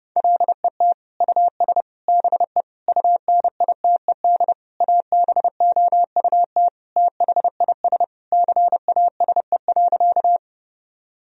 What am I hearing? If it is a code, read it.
Morse code, heard as a keyed tone, about 32 wpm, 700 Hz. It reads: LET US BE UNITED ABOUT THIS CASE.
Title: Sentences from Top 1000 Words - Review Part 4 - 30wpm